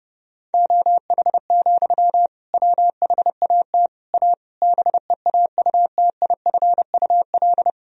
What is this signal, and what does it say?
Morse code, heard as a keyed tone, about 30 wpm, 700 Hz. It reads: OH, WHAT A BEAUTIFUL
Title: Sentences from Top 1000 Words - Review Part 2 - 30wpm